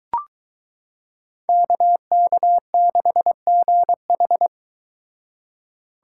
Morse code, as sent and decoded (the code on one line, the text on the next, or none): -.- -.- -.... --. ....
KK6GH